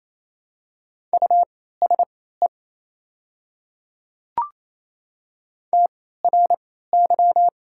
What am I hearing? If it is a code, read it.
Morse code, heard as a keyed tone, about 28 wpm, spaced as for 16 wpm, 700 Hz. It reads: USE TRY
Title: 3x Character Spacing - Top 300 Words - 28wpm